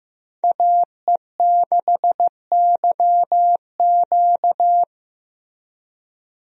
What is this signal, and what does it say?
Morse code, heard as a keyed tone, about 15 wpm, 700 Hz. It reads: AE6YQ
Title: USA Call Signs - 5 Characters - Repeated 3x - 15wpm